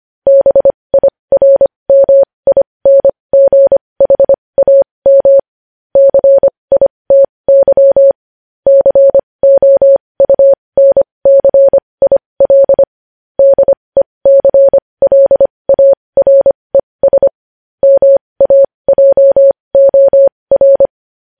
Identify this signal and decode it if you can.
Morse code, heard as a keyed tone, about 25 wpm, spaced as for 20 wpm, 550 Hz. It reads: BIRMINGHAM CITY COUNCIL DECLARES MAJOR
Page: News Headlines in Morse Code at 20 WPM